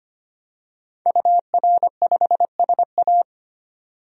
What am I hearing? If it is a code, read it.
Morse code, heard as a keyed tone, about 25 wpm, 700 Hz. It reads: UR5SA
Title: DXCC-100 Call Signs - 5 Characters - 25wpm